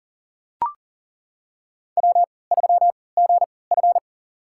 Morse code, heard as a keyed tone, about 40 wpm, spaced as for 23 wpm, 700 Hz. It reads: W3GF
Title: USA Call Signs - 4 Characters - 3x Character Spacing - 40wpm